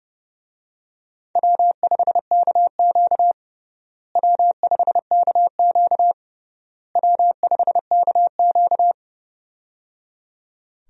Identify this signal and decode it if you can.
Morse code, heard as a keyed tone, about 30 wpm, 700 Hz. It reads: W5KQ W5KQ W5KQ
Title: USA Call Signs - 4 Characters - Repeated 3x - 30wpm